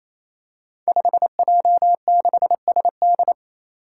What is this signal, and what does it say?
Morse code, heard as a keyed tone, about 28 wpm, 700 Hz. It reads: 5J6SD
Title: Five Challenging Letter-Number Combinations 28wpm